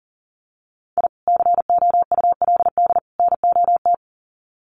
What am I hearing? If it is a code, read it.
Morse code, heard as a keyed tone, about 40 wpm, 700 Hz. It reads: I COULD NOT
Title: Sets of 3 Words 40wpm